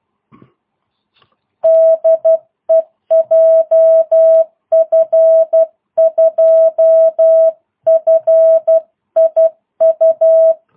Morse code, heard as a keyed tone, about 12 wpm, 650 Hz. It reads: DE JF2FIU